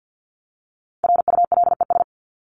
Morse code, heard as a keyed tone, about 50 wpm, 700 Hz. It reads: RULES
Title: ES:S Suffix - Top 100 Words with Base Words 50wpm